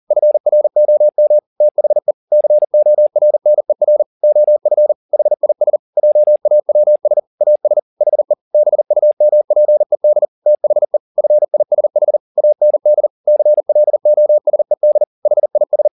Morse code, heard as a keyed tone, about 40 wpm, 600 Hz. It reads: FROM THE CORNER OF HIS JAWS AS HE BUMPED THE FISH AND CLOSED HIS
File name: the old man and the sea 2 40wpm